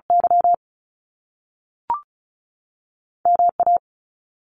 Morse code, heard as a keyed tone, about 36 wpm, 700 Hz. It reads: Y M A
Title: Top Two Letter Combinations in Words 35wpm